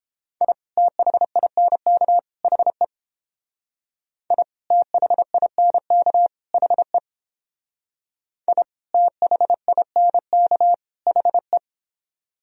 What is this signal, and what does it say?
Morse code, heard as a keyed tone, about 29 wpm, 700 Hz. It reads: I THINK HE I THINK HE I THINK HE